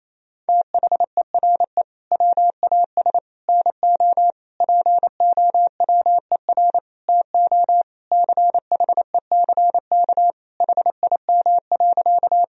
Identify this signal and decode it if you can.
Morse code, heard as a keyed tone, about 28 wpm, 700 Hz. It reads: THERE WAS NO POWER TO CHECK HIM.
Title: Sentences from Top 1000 Words - Review Part 3 - 28wpm